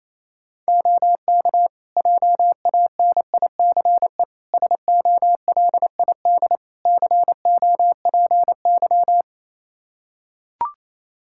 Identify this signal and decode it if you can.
Morse code, heard as a keyed tone, about 28 wpm, 700 Hz. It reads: OK JANICE SOLID COPY
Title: Practice QSO - Exchange 3 - More Info - 28wpm